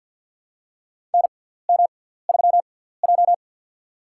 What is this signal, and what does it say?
Morse code, heard as a keyed tone, about 50 wpm, spaced as for 17 wpm, 700 Hz. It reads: NM3J
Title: USA Call Signs - 4 Characters - 6x Character Spacing - 50wpm